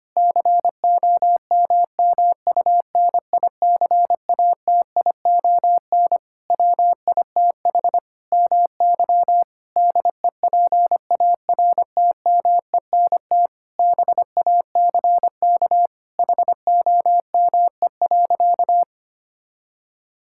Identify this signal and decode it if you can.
Morse code, heard as a keyed tone, about 25 wpm, 700 Hz. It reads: COMMUNICATION WITH MY DEPARTMENT BACK HOME.
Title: Sentences from Top 1000 Words - Review Part 6 - 25wpm